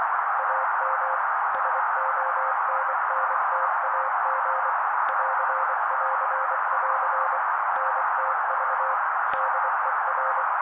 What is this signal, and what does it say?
Morse code, heard as a keyed tone, about 24 wpm, 530 Hz. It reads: AM SONNTAG <AR><AR><AR> NTV DER
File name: daily / 23wpm noise